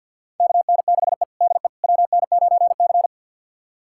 Morse code, heard as a keyed tone, about 51 wpm, 690 Hz. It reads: KN6E DE WN9C
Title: USA Callsign DE Callsign - 4 Characters - 50wpm